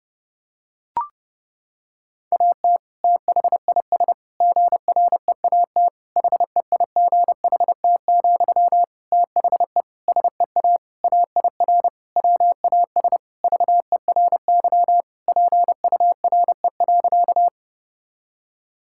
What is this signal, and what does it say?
Morse code, heard as a keyed tone, about 30 wpm, 700 Hz. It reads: AT THIS GREAT HEIGHT, THE SEA AIR WAS VERY PURE.
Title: Sentences from Top 1600 Words 30wpm